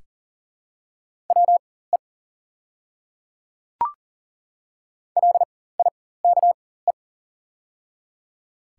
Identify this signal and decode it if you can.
Morse code, heard as a keyed tone, about 42 wpm, spaced as for 19 wpm, 690 Hz. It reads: WE LIKE
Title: 4x Character Spacing - Top 200 Words - 40wpm